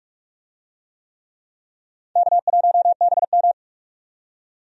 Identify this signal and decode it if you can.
Morse code, heard as a keyed tone, about 46 wpm, 700 Hz. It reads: K1BM
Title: USA Call Signs - 4 Characters - 45wpm